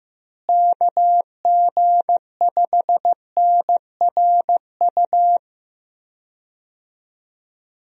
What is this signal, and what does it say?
Morse code, heard as a keyed tone, about 14 wpm, 700 Hz. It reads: KG5NRU
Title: USA Call Signs - 6 Characters - Repeated 3x - 15wpm